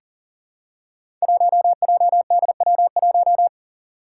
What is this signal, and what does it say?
Morse code, heard as a keyed tone, about 39 wpm, 700 Hz. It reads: 1JDW1
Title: Five Challenging Letter-Number Combinations 40wpm